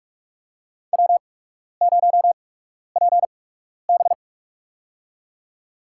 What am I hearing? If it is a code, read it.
Morse code, heard as a keyed tone, about 43 wpm, spaced as for 12 wpm, 700 Hz. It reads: W0PB
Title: USA Call Signs - 4 Characters - 8x Character Spacing - 45wpm